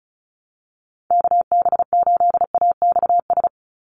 Morse code, heard as a keyed tone, about 34 wpm, 700 Hz. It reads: KB8AXS
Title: USA Call Signs 35wpm